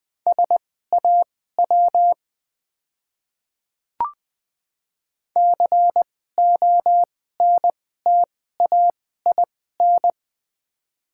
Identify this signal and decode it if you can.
Morse code, heard as a keyed tone, about 20 wpm, spaced as for 15 wpm, 690 Hz. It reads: SAW CONTAIN
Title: 2x Character Spacing - Top 500 Words - 20wpm